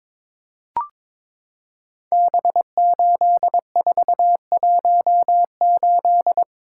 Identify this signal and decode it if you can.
Morse code, heard as a keyed tone, about 22 wpm, 700 Hz. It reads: B8418
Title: Code Groups - 5 Letter:Numbers - 22wpm